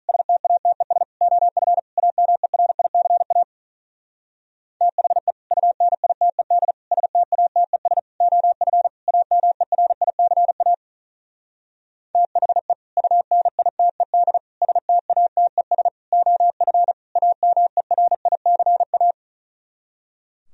Morse code, heard as a keyed tone, about 43 wpm, 700 Hz. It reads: STATES OF AMERICA THE UNITED STATES OF AMERICA THE UNITED STATES OF AMERICA
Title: Speed-Racing - Sets of 5 Words 35wpm